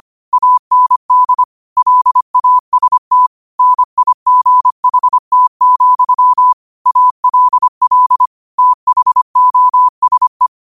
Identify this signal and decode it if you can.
Morse code, heard as a keyed tone, about 25 wpm, 1000 Hz. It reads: AND LAST NIGHT, ALL THOSE
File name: qotdfiles / 25wpm